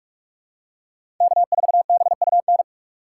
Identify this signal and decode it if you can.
Morse code, heard as a keyed tone, about 44 wpm, 690 Hz. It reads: K4BUN
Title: USA Call Signs - 6x Character Spacing - 45wpm